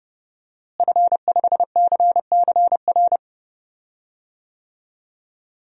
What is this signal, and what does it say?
Morse code, heard as a keyed tone, about 30 wpm, 700 Hz. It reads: F5CCR
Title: DXCC-100 Call Signs - 5 Characters - Repeated 3x - 30wpm